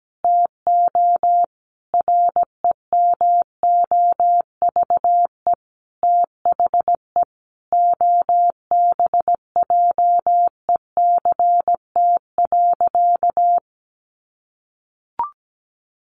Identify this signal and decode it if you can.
Morse code, heard as a keyed tone, about 17 wpm, 700 Hz. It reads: TO REMOVE THE OBJECT.
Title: Sentences from Top 800 Words 17wpm